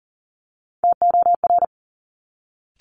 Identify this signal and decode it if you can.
Morse code, heard as a keyed tone, about 42 wpm, 700 Hz. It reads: TOR
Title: Top Three Letter Combinations in Words 40wpm